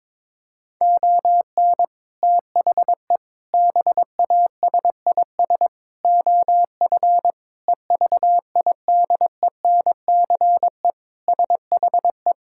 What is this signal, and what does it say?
Morse code, heard as a keyed tone, about 22 wpm, 700 Hz. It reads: ON THE BASIS OF EVIDENCE SHE
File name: practice / 22wpm / Sentences with 4-Word Phrases - 22wpm